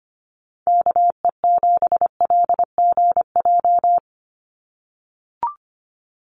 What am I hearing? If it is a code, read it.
Morse code, heard as a keyed tone, about 25 wpm, 700 Hz. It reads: KE7LGJ
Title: USA Call Signs 25wpm